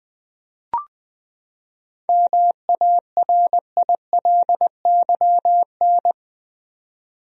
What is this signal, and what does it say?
Morse code, heard as a keyed tone, about 20 wpm, 700 Hz. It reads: MARILYN
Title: USA Common Names - 20wpm